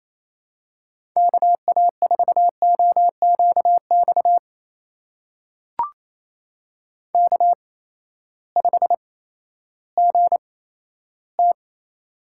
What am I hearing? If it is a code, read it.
Morse code, heard as a keyed tone, about 28 wpm, spaced as for 14 wpm, 700 Hz. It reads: KA4OQX K5GT
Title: USA Call Signs - 8x Character Spacing - 28wpm